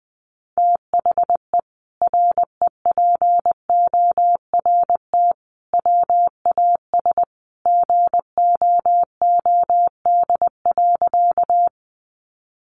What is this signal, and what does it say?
Morse code, heard as a keyed tone, about 20 wpm, 700 Hz. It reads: THE REPORT WAS GOOD.
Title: Sentences from Top 900 Words 20wpm